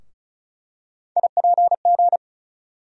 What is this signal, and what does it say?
Morse code, heard as a keyed tone, about 34 wpm, 700 Hz. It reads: IPG
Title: USA Call Sign Postfixes - 3 Letters 35wpm